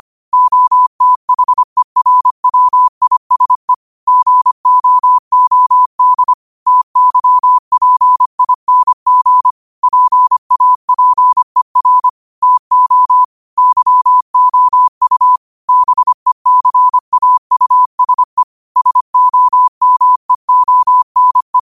Morse code, heard as a keyed tone, about 25 wpm, 1000 Hz. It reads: OTHERWISE GOOD TYPING PAPER TO YOU BECAUSE SOMEONE
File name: qotdfiles / 25wpm